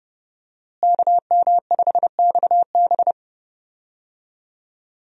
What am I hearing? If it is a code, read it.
Morse code, heard as a keyed tone, about 32 wpm, 700 Hz. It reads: KM5XB
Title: USA Call Signs - 5 Characters - Repeated 3x - 30wpm